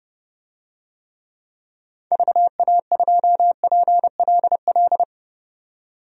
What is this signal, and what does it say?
Morse code, heard as a keyed tone, about 30 wpm, 700 Hz. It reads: VA2PLL
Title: DXCC-100 Call Signs - 6 Characters - 30wpm